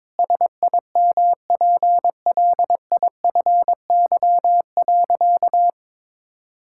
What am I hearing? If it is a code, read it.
Morse code, heard as a keyed tone, about 22 wpm, 700 Hz. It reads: SIMPLIFY.
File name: Sentences with 7-Word Phrases - 22wpm